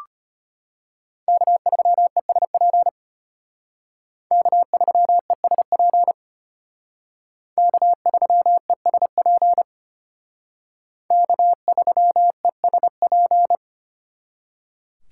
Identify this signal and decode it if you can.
Morse code, heard as a keyed tone, about 32 wpm, 700 Hz. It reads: K3ESP K3ESP K3ESP K3ESP